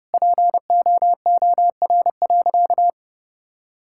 Morse code, heard as a keyed tone, about 30 wpm, 700 Hz. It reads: POOR.
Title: Sentences from Top 1000 Words - Review Part 6 - 30wpm